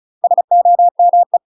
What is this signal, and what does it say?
Morse code, heard as a keyed tone, about 36 wpm, 700 Hz. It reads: SOME
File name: UK MASTER QSOs TRAINING / 35WPM / UK MASTER QSOs 0029